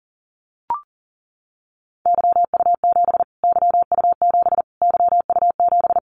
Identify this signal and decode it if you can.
Morse code, heard as a keyed tone, about 37 wpm, 700 Hz. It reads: YU7 YU7 YU7
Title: Callsign Prefixes for Top 25 DXCC - Repeated 3x - 40wpm